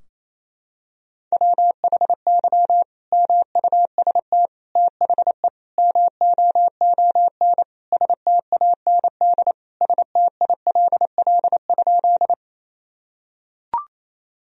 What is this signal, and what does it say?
Morse code, heard as a keyed tone, about 28 wpm, 700 Hz. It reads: WHY MUST THE MOON STAND STILL?
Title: Sentences from Top 1000 Words - Review Part 3 - 28wpm